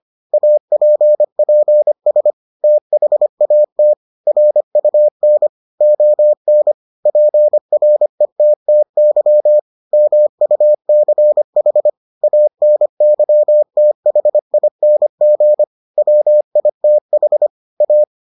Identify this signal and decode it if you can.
Morse code, heard as a keyed tone, about 25 wpm, 600 Hz. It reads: APPS THAT RUN ON PRETTY MUCH ANYTHING WITH A